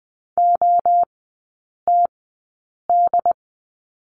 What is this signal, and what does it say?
Morse code, heard as a keyed tone, about 20 wpm, spaced as for 8 wpm, 700 Hz. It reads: OTD
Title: 093 Lesson 16e - Letter c - Three Characters 20wpm